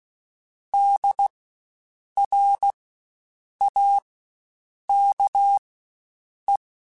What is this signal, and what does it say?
Morse code, heard as a keyed tone, about 16 wpm, spaced as for 7 wpm, 800 Hz. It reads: DRAKE
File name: Spaced Alphas / Spaced Alphas mp3s / Ds sp @15